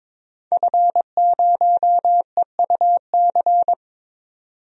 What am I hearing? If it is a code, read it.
Morse code, heard as a keyed tone, about 22 wpm, 700 Hz. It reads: F0EUC
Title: DXCC-100 Call Signs - 5 Characters - 22wpm